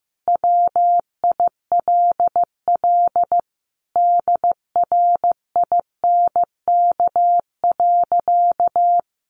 Morse code, heard as a keyed tone, about 15 wpm, 700 Hz. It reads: WILL DRINK.
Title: Sentences from Top 1500 Words 15wpm